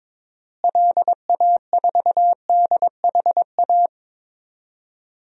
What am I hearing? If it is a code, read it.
Morse code, heard as a keyed tone, about 22 wpm, 700 Hz. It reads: LA4DHA
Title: DXCC-100 Call Signs - 6 Characters - Repeated 3x - 22wpm